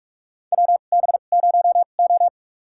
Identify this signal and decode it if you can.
Morse code, heard as a keyed tone, about 46 wpm, spaced as for 33 wpm, 700 Hz. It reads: WB0O
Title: USA Call Signs - 4 Characters - 2x Character Spacing - 45wpm